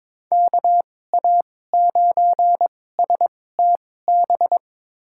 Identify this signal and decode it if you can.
Morse code, heard as a keyed tone, about 22 wpm, spaced as for 16 wpm, 700 Hz. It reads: KA9STB
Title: USA Call Signs - 6 Characters - 2x Character Spacing - 22wpm